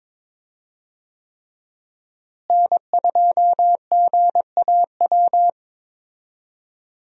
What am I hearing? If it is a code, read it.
Morse code, heard as a keyed tone, about 22 wpm, 700 Hz. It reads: N2GAW
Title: USA Call Signs - 5 Characters - Repeated 3x - 22wpm